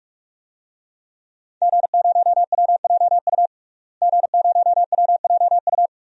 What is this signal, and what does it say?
Morse code, heard as a keyed tone, about 41 wpm, 700 Hz. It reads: G0WJU G0WJU
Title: DXCC-100 Call Signs - 5 Characters - Repeated 3x - 45wpm